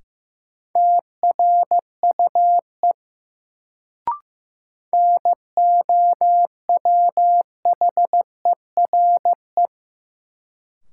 Morse code, heard as a keyed tone, about 16 wpm, 700 Hz. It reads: TRUE NOWHERE
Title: Sets of 5 Words Encore - New Word Drill - 15wpm